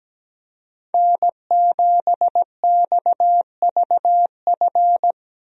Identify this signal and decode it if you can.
Morse code, heard as a keyed tone, about 17 wpm, 700 Hz. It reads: N7XVF
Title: USA Call Signs - 5 Characters - Repeated 2x - 17wpm